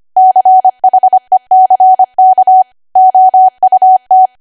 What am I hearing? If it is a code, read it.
Morse code, heard as a keyed tone, about 25 wpm, 750 Hz. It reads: CHECK OUT
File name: Archive / 25 WPM / 25WPM